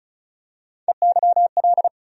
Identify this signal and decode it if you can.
Morse code, heard as a keyed tone, about 36 wpm, 700 Hz. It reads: EYL
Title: USA Call Sign Postfixes - 3 Letters 35wpm